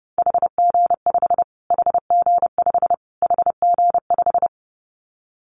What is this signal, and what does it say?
Morse code, heard as a keyed tone, about 30 wpm, 700 Hz. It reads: HG5 HG5 HG5
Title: Callsign Prefixes for Top 25 DXCC - Repeated 3x - 30wpm